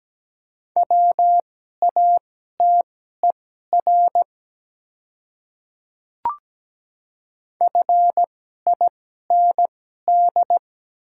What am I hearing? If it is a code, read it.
Morse code, heard as a keyed tone, about 17 wpm, spaced as for 12 wpm, 700 Hz. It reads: WATER FIND
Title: 2x Character Spacing - Top 100 Words - 17wpm